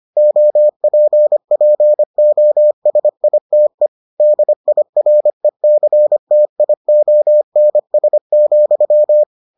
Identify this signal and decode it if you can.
Morse code, heard as a keyed tone, about 25 wpm, 600 Hz. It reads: OPPOSITE DIRECTIONS,